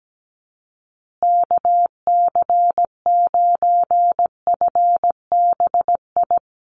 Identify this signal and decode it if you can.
Morse code, heard as a keyed tone, about 17 wpm, 700 Hz. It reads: KC9FBI
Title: USA Call Signs Repeated 2x 17wpm